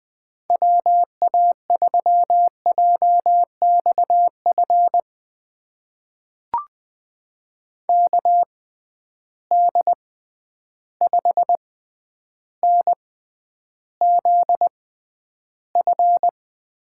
Morse code, heard as a keyed tone, about 20 wpm, spaced as for 10 wpm, 700 Hz. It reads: WA3JXF KD5NZF
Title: USA Call Signs - 6 Characters - 6x Character Spacing - 20wpm